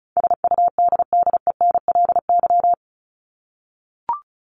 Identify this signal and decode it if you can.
Morse code, heard as a keyed tone, about 34 wpm, 700 Hz. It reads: SUDDENLY
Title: LY Suffix - Top 100 Words with Base Words 35wpm